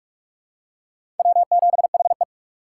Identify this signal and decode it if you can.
Morse code, heard as a keyed tone, about 46 wpm, 700 Hz. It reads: W7HE
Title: USA Call Signs - 4 Characters - 2x Character Spacing - 45wpm